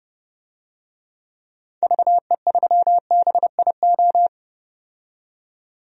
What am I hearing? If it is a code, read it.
Morse code, heard as a keyed tone, about 30 wpm, 700 Hz. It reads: VE3BIO
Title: DXCC-100 Call Signs - 6 Characters - 30wpm